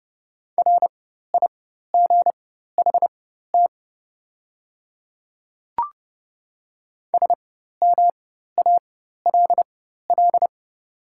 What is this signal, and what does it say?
Morse code, heard as a keyed tone, about 30 wpm, spaced as for 14 wpm, 700 Hz. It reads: RIGHT SMALL
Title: 4x Character Spacing - Top 200 Words - 30wpm